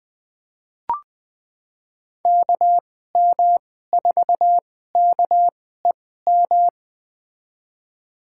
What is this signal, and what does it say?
Morse code, heard as a keyed tone, about 20 wpm, spaced as for 14 wpm, 700 Hz. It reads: KM4KEM
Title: USA Call Signs - 6 Characters - 2x Character Spacing - 20wpm